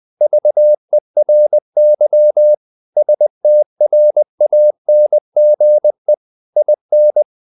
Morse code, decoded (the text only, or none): VERY STRANGE IN